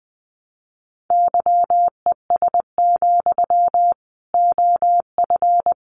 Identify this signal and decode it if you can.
Morse code, heard as a keyed tone, about 20 wpm, 700 Hz. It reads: YES, OF
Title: Sentences from Top 500 Words 20wpm